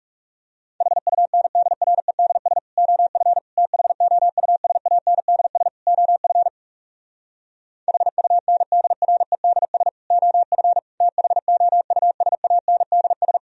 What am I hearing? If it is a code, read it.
Morse code, heard as a keyed tone, about 40 wpm, 700 Hz. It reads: HUNDREDS OF THOUSANDS OF HUNDREDS OF THOUSANDS